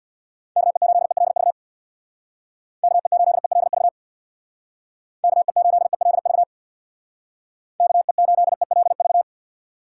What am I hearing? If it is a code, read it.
Morse code, heard as a keyed tone, about 62 wpm, 700 Hz. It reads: KE8ELV KE8ELV KE8ELV KE8ELV
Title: Speed Racing - USA 2x3 Call Signs 50wpm